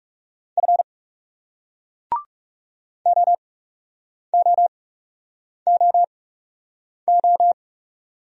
Morse code, heard as a keyed tone, about 37 wpm, spaced as for 8 wpm, 700 Hz. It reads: F OOOO
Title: Speed-Racing Single Letters 30wpm